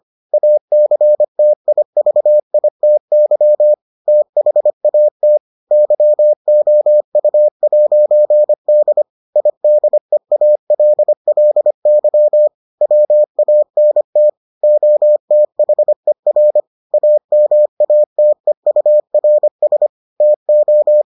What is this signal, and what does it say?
Morse code, heard as a keyed tone, about 25 wpm, 600 Hz. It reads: ACTIVITY THAT YOU'D IDEALLY WANT OTHER AMATEURS TO